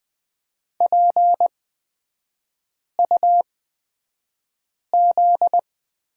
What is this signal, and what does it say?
Morse code, heard as a keyed tone, about 20 wpm, spaced as for 5 wpm, 700 Hz. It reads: PUZ
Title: 243 Lesson 39e - Letter z - Three Characters 20wpm